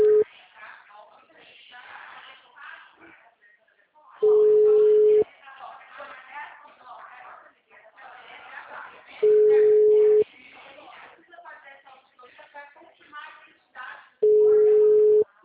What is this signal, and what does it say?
Ringback tone (Europe, 425 Hz), with speech.